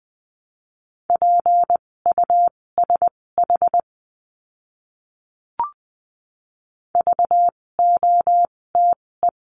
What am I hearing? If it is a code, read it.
Morse code, heard as a keyed tone, about 20 wpm, spaced as for 16 wpm, 700 Hz. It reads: PUSH VOTE
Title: 256 Lesson 42a - Words 12-20wpm